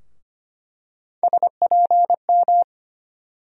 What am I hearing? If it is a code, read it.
Morse code, heard as a keyed tone, about 25 wpm, 700 Hz. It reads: SPM
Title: USA Call Sign Postfixes - 3 Letters 25wpm